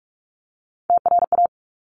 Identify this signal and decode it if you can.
Morse code, heard as a keyed tone, about 46 wpm, 690 Hz. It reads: TRA
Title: Top Three Letter Combinations in Words 45wpm